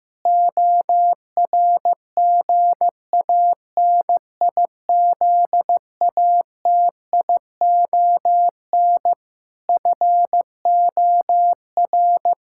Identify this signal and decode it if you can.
Morse code, heard as a keyed tone, about 15 wpm, 700 Hz. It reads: ORGANIZATION FOR
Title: Sentences from Top 1000 Words - Review Part 5 - 15wpm